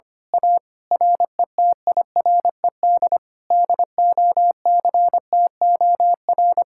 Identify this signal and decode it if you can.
Morse code, heard as a keyed tone, about 25 wpm, 700 Hz. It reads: A RETIRED DOCTOR
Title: Speed-Racing Common QSO Phrases Encore - 25wpm